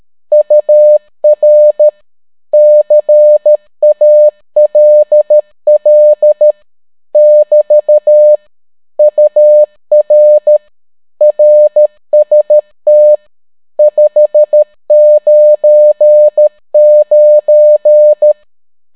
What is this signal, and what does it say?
Morse code, heard as a keyed tone, about 13 wpm, 600 Hz. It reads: UR CALL <BT> UR RST 599